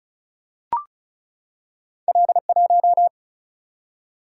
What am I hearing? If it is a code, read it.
Morse code, heard as a keyed tone, about 36 wpm, 700 Hz. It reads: L 1
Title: Single-Letter Single-Number 35wpm